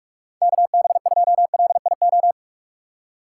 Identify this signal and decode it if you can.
Morse code, heard as a keyed tone, about 44 wpm, 700 Hz. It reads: KB2LIO
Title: USA Call Signs - 6 Characters - 6x Character Spacing - 45wpm